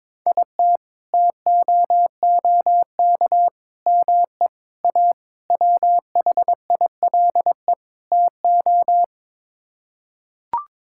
Morse code, heard as a keyed tone, about 21 wpm, 700 Hz. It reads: IT TOOK ME A WHILE TO
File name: practice / 22wpm / Sets of 6 Words - 22wpm